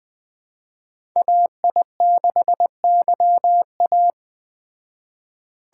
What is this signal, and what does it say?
Morse code, heard as a keyed tone, about 20 wpm, 700 Hz. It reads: AI6YA
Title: Speed Racing - USA Call Signs 17wpm